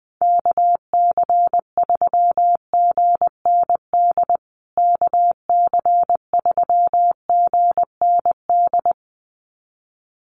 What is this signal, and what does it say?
Morse code, heard as a keyed tone, about 20 wpm, 700 Hz. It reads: KC3GND KC3GND
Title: USA Call Signs Repeated 2x 20wpm